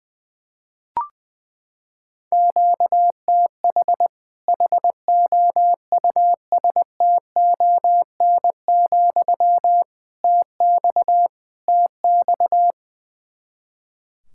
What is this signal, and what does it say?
Morse code, heard as a keyed tone, about 21 wpm, 700 Hz. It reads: QTH HOUSTON, TX TX
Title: Practice QSO - Exchange 1 - Beginning - 20wpm